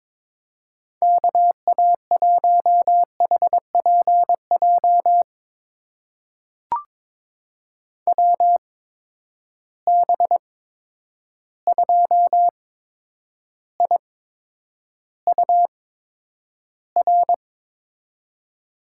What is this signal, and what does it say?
Morse code, heard as a keyed tone, about 22 wpm, spaced as for 9 wpm, 700 Hz. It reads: KA1HPJ WB2IUR
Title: USA Call Signs - 6 Characters - 8x Character Spacing - 22wpm